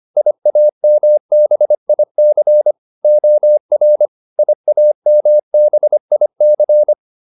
Morse code, heard as a keyed tone, about 25 wpm, 600 Hz. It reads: IAMBIC OR IAMBIC